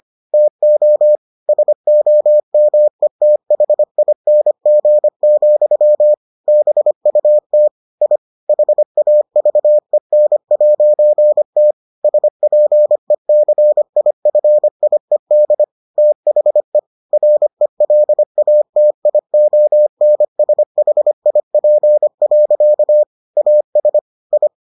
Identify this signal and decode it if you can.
Morse code, heard as a keyed tone, about 25 wpm, 600 Hz. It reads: TO SOMETHING, BUT I HAVEN'T SPECIFIED THE RELATIONSHIP. AS I